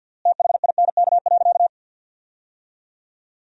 Morse code, heard as a keyed tone, about 50 wpm, 700 Hz. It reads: THINK.
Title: 3x Word-Spacing - Sentences from Top 500 Words - 50wpm